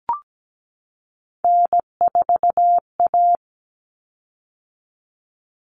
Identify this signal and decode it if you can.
Morse code, heard as a keyed tone, about 17 wpm, 700 Hz. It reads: N4A
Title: Common Cut Numbers - 3-Digit 17wpm